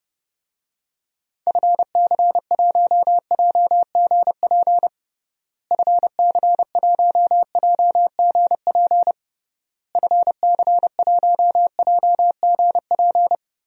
Morse code, heard as a keyed tone, about 32 wpm, 700 Hz. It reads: FC1JGP FC1JGP FC1JGP
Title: DXCC-100 Call Signs - 6 Characters - Repeated 3x - 30wpm